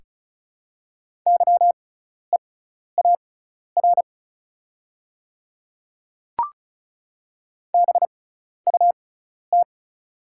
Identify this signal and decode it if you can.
Morse code, heard as a keyed tone, about 36 wpm, spaced as for 12 wpm, 700 Hz. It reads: YEAR BUT
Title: 6x Character Spacing - Top 200 Words - 35wpm